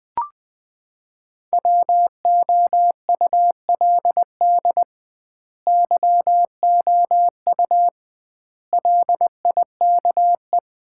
Morse code, heard as a keyed tone, about 21 wpm, 700 Hz. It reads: WOULD YOU LIKE
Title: 2x Word-Spacing - Sets of 4 Words - 20wpm